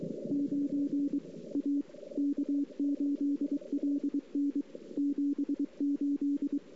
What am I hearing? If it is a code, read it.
Morse code, heard as a keyed tone, about 22 wpm, 280 Hz. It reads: 9 A K8LN 78
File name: CW / 2008-ss-cw-k5tr-op-n5rz -23:00